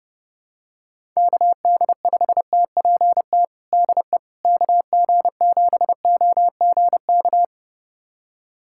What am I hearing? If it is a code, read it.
Morse code, heard as a keyed tone, about 32 wpm, 700 Hz. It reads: KD5TPT DE KG7OGK
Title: USA Callsign DE Callsign 30wpm